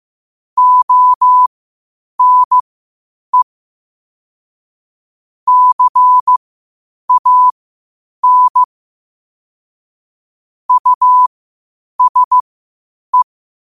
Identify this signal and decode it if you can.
Morse code, heard as a keyed tone, about 15 wpm, spaced as for 8 wpm, 1000 Hz. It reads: ONE CAN USE
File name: qotdfiles / 8wpm